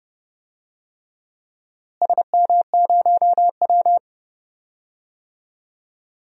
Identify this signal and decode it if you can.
Morse code, heard as a keyed tone, about 30 wpm, 700 Hz. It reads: SM0W
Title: DXCC-100 Call Signs - 4 Characters - Repeated 3x - 30wpm